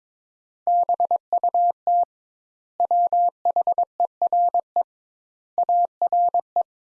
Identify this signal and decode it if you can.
Morse code, heard as a keyed tone, about 21 wpm, 700 Hz. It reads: BUT WHERE ARE
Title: Sentences Top 200 Words - 22wpm